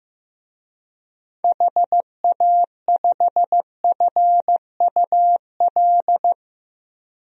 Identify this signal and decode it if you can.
Morse code, heard as a keyed tone, about 16 wpm, 700 Hz. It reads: HA5FUL
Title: DXCC-100 Call Signs - 6 Characters - 15wpm